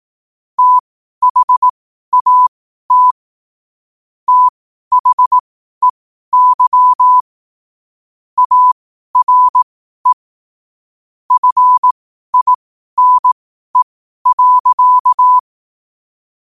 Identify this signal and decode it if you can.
Morse code, heard as a keyed tone, about 18 wpm, spaced as for 12 wpm, 1000 Hz. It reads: THAT THEY ARE FINE.
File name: qotdfiles / 12wpm